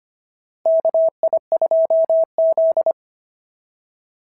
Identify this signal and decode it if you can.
Morse code, heard as a keyed tone, about 25 wpm, 690 Hz. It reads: KI2Z
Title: SOTA QSO Practice 25wpm